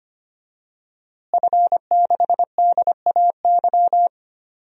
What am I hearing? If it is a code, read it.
Morse code, heard as a keyed tone, about 25 wpm, 700 Hz. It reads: F6DAY
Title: DXCC-100 Call Signs - 5 Characters - Repeated 3x - 25wpm